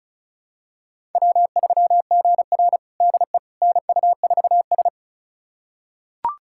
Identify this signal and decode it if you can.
Morse code, heard as a keyed tone, about 36 wpm, 700 Hz. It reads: W3GR DE NU4S
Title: USA Callsign DE Callsign - 4 Characters - 35wpm